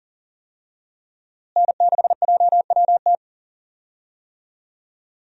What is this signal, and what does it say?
Morse code, heard as a keyed tone, about 40 wpm, 700 Hz. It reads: N6JWT
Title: USA Call Signs - 5 Characters - Repeated 2x - 40wpm